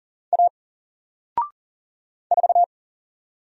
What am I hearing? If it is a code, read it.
Morse code, heard as a keyed tone, about 41 wpm, 700 Hz. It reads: A 4
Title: Single Letter-Number Alt 40wpm